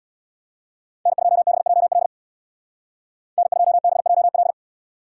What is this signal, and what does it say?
Morse code, heard as a keyed tone, about 65 wpm, 700 Hz. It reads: N2BJL N2BJL